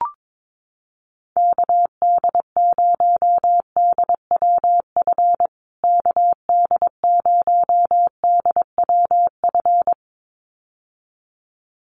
Morse code, heard as a keyed tone, about 22 wpm, 700 Hz. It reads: KD0DWF KD0DWF
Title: USA Call Signs Repeated 2x 22wpm